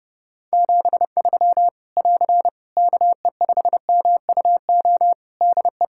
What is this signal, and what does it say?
Morse code, heard as a keyed tone, about 30 wpm, 700 Hz. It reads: 73 <AR> KE5MUO DE
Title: Practice QSO - Exchange 4 - Ending - 30wpm